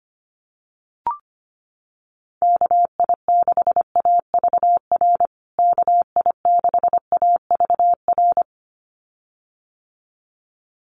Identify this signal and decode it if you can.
Morse code, heard as a keyed tone, about 25 wpm, 700 Hz. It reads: KI6AVR KI6AVR
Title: USA Call Signs Repeated 2x 25wpm